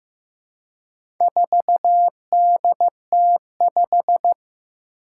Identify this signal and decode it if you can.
Morse code, heard as a keyed tone, about 15 wpm, 700 Hz. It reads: 4DT5
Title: Four Challenging Letter-Number Combinations 15wpm